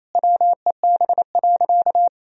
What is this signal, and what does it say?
Morse code, heard as a keyed tone, about 28 wpm, 700 Hz. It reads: WEB.
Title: Sentences from Top 1000 Words - Review Part 4 - 28wpm